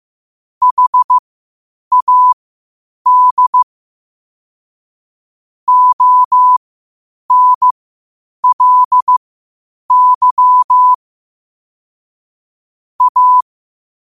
Morse code, heard as a keyed tone, about 15 wpm, spaced as for 8 wpm, 1000 Hz. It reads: HAD ONLY A
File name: qotdfiles / 8wpm